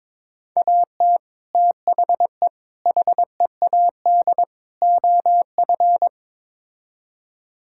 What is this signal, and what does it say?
Morse code, heard as a keyed tone, about 22 wpm, 700 Hz. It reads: AT THE HEAD OF
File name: practice / 22wpm / Build A Phrase - 5-Word Phrases - 22wpm